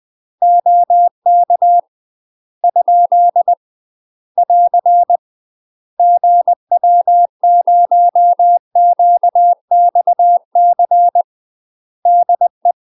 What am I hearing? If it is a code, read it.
Morse code, heard as a keyed tone, about 20 wpm, 700 Hz. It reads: OK ? <AR> GW0QXC DE
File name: UK MASTER QSOs TRAINING/20WPM